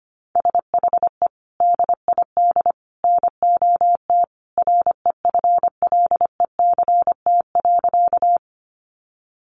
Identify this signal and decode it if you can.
Morse code, heard as a keyed tone, about 25 wpm, 700 Hz. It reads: SHE DID NOT REFLECT.
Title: Sentences from Top 1500 Words 25wpm